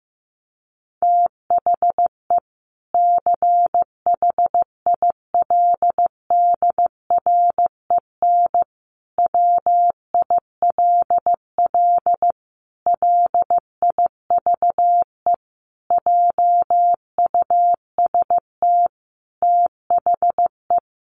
Morse code, heard as a keyed tone, about 15 wpm, 700 Hz. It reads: THE CHILDREN WILL LIVE JUST THE
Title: Sentences from Top 300 Words 15wpm